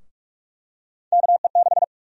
Morse code, heard as a keyed tone, about 44 wpm, 700 Hz. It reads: KE6
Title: USA Call Sign Prefixes - Repeated 2x - 45wpm